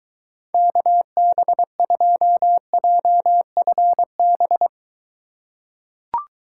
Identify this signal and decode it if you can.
Morse code, heard as a keyed tone, about 21 wpm, 700 Hz. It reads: KB2JFB